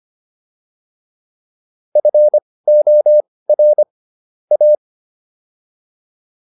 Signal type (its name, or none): Morse code